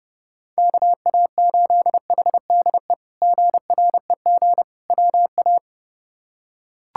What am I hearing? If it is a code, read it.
Morse code, heard as a keyed tone, about 32 wpm, 700 Hz. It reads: KA8HDE GREG WA
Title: K1USN SST QSO Practice - 30wpm